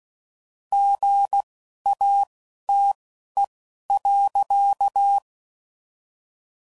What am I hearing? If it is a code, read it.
Morse code, heard as a keyed tone, about 16 wpm, spaced as for 11 wpm, 800 Hz. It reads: GATE.
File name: CW Code Groups @10WPM / Technical Subjects / OR Gate